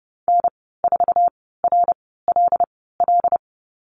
Morse code, heard as a keyed tone, about 30 wpm, spaced as for 17 wpm, 700 Hz. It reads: N4RLL
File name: Lesson 15g - Number 5 - Call Signs 30wpm